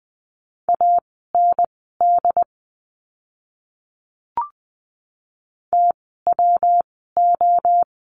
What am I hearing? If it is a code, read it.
Morse code, heard as a keyed tone, about 20 wpm, spaced as for 15 wpm, 700 Hz. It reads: AND TWO
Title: 237 Lesson 38f - Letter q - Words 20wpm